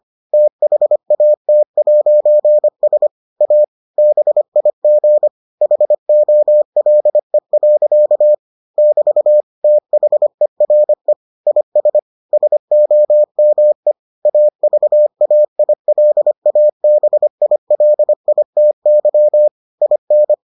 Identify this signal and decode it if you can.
Morse code, heard as a keyed tone, about 25 wpm, 600 Hz. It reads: THAT'S A BIG HOLE. <BT> THERE IS SOME AVAILABILITY IN